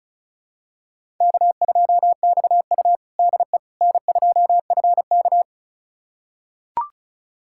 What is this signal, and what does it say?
Morse code, heard as a keyed tone, about 36 wpm, 700 Hz. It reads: K2XU DE N2FK
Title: USA Callsign DE Callsign - 4 Characters - 35wpm